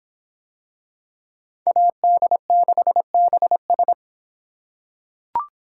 Morse code, heard as a keyed tone, about 25 wpm, 700 Hz. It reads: AD6BS